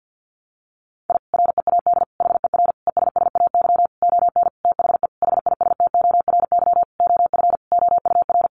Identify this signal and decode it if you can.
Morse code, heard as a keyed tone, about 50 wpm, 700 Hz. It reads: I READ HER ESSAY ON THE HISTORY OF OUR
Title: Sentences from Top 1500 Words 50wpm